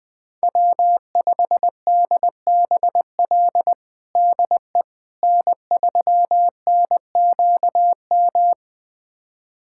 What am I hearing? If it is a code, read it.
Morse code, heard as a keyed tone, about 20 wpm, 700 Hz. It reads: W5DBL DE N3NQM
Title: USA Callsign DE Callsign - 5 Characters - 20wpm